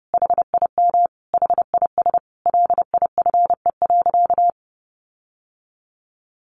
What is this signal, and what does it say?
Morse code, heard as a keyed tone, about 28 wpm, 700 Hz. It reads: HIM HIS LIFE.
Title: Sentences from Top 1500 Words 30wpm